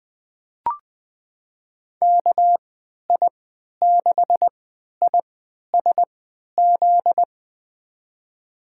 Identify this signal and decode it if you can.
Morse code, heard as a keyed tone, about 21 wpm, spaced as for 12 wpm, 700 Hz. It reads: KI6ISZ
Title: USA Call Signs - 6 Characters - 3x Character Spacing - 20wpm